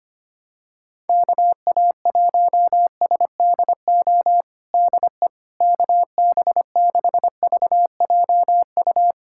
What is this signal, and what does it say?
Morse code, heard as a keyed tone, about 25 wpm, 700 Hz. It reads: KA1SDO DE KB6VJU
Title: USA Callsign DE Callsign - 6 Characters - 25wpm